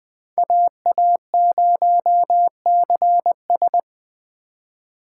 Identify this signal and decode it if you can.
Morse code, heard as a keyed tone, about 21 wpm, 700 Hz. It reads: AA0CS
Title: Speed Racing - USA 2x2 Call Signs 17wpm